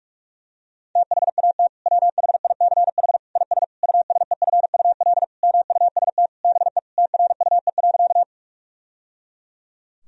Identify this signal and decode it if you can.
Morse code, heard as a keyed tone, about 46 wpm, 700 Hz. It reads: THAT WHICH IS USEFUL MUST BE TRUE.
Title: Sentences from Top 1000 Words - Review Part 4 - 45wpm